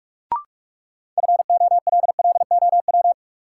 Morse code, heard as a keyed tone, about 44 wpm, 700 Hz. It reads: FOLLOW
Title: Sentences from Top 200 Words - New Word Drill - 45wpm